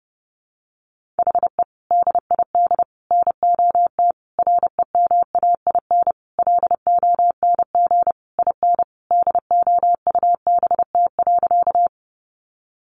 Morse code, heard as a keyed tone, about 30 wpm, 700 Hz. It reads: HE DID NOT REMAIN LONG IN DOUBT.
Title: Sentences from Top 1500 Words 30wpm